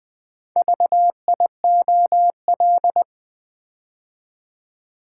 Morse code, heard as a keyed tone, about 20 wpm, 690 Hz. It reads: VIOL
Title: Build A Word - 2 Letters At A Time - 8-Letter Words - 20wpm